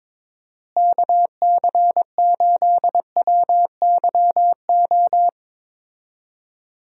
Morse code, heard as a keyed tone, about 21 wpm, 700 Hz. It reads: KC8WYO
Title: USA Call Signs - 6 Characters - Repeated 2x - 22wpm